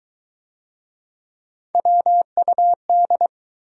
Morse code, heard as a keyed tone, about 21 wpm, 700 Hz. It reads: WUD